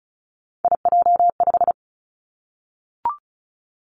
Morse code, heard as a keyed tone, about 36 wpm, 700 Hz. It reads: IJ5
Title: Three Challenging Letter-Number Combinations 35wpm